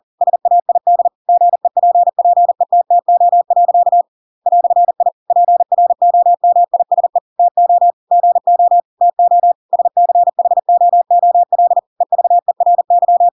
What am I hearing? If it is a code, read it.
Morse code, heard as a keyed tone, about 40 wpm, 700 Hz. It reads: SAID GEPPETTO. "I PROMISE TO GO TO SCHOOL EVERY